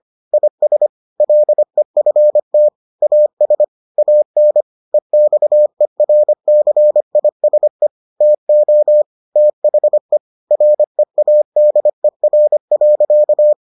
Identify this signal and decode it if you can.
Morse code, heard as a keyed tone, about 25 wpm, 600 Hz. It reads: IS LEFT AS AN EXERCISE TO THE READER.